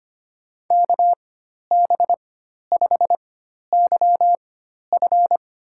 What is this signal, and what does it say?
Morse code, heard as a keyed tone, about 25 wpm, spaced as for 12 wpm, 700 Hz. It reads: KB5YF
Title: USA Call Signs - 4x Character Spacing - 25wpm